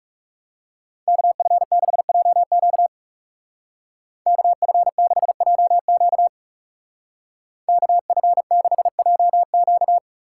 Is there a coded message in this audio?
Yes: Morse code from 1.1 to 10.0 s, heard as a keyed tone, about 38 wpm, 700 Hz. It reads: KF6JQ KF6JQ KF6JQ